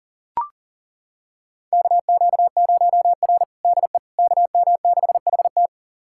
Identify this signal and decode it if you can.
Morse code, heard as a keyed tone, about 40 wpm, 700 Hz. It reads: KQ0R DE KM6HT
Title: USA Callsign DE Callsign 40wpm